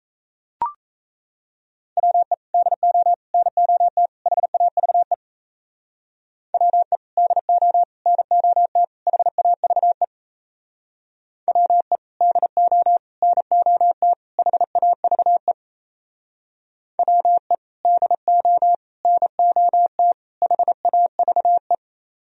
Morse code, heard as a keyed tone, about 33 wpm, 700 Hz. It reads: WE DO NOT HAVE WE DO NOT HAVE WE DO NOT HAVE WE DO NOT HAVE
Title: Speed-Racing - Sets of 4 Words 28wpm